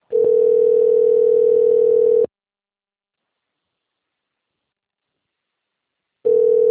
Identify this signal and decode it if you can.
Ringback tone (North America), 440 + 480 Hz.